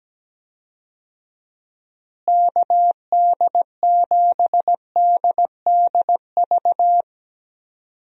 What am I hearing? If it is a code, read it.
Morse code, heard as a keyed tone, about 17 wpm, 700 Hz. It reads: KD7DDV
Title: USA Call Signs - 6 Characters - Repeated 2x - 17wpm